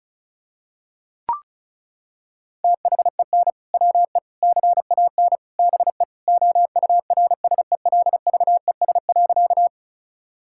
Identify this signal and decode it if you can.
Morse code, heard as a keyed tone, about 36 wpm, 700 Hz. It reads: THEN WE CAN BE OURSELVES.
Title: Sentences from Top 1600 Words 35wpm